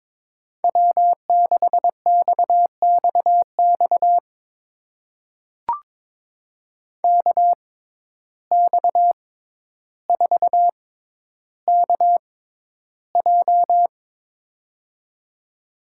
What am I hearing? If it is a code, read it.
Morse code, heard as a keyed tone, about 22 wpm, spaced as for 11 wpm, 700 Hz. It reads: W6XXX KX4KJ
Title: USA Call Signs - 5 Characters - 6x Character Spacing - 22wpm